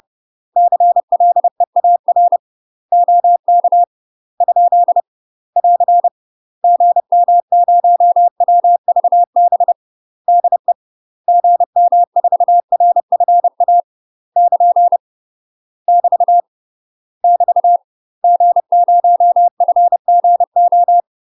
Morse code, heard as a keyed tone, about 30 wpm, 700 Hz. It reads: CLEAR OK ? <AR> GM0WVB DE GM4RFA <KN> <BT> <BT> G0FGO
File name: UK MASTER QSOs TRAINING / 30WPM / UK MASTER QSOs 0040